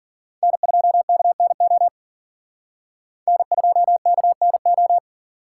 Morse code, heard as a keyed tone, about 50 wpm, 700 Hz. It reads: N2KNO N2KNO